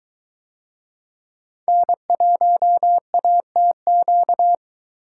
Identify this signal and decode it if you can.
Morse code, heard as a keyed tone, about 23 wpm, 700 Hz. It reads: N1ATQ